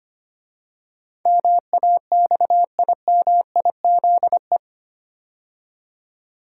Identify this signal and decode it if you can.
Morse code, heard as a keyed tone, about 25 wpm, 700 Hz. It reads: MAXIMIZE
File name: 8 Letter Words - First 3 Letters Given - 25wpm